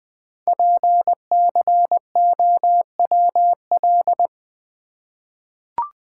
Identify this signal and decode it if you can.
Morse code, heard as a keyed tone, about 20 wpm, 700 Hz. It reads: PCOWL
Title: Code Groups - 5 Letters - 20wpm